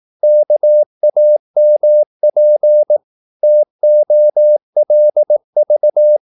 Morse code, heard as a keyed tone, about 18 wpm, 600 Hz. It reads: KAMP TOLV
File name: Korte ord DK 18wpm